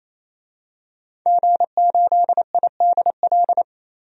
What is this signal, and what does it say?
Morse code, heard as a keyed tone, about 28 wpm, 700 Hz. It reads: G8IDL
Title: DXCC-100 Call Signs - 5 Characters - 28wpm